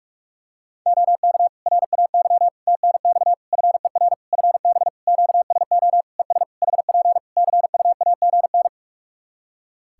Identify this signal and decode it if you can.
Morse code, heard as a keyed tone, about 41 wpm, 700 Hz. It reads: OK RAY TNX FER FB QSO ES HP CUAGN
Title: Practice QSO - Exchange 4 - Ending - 45wpm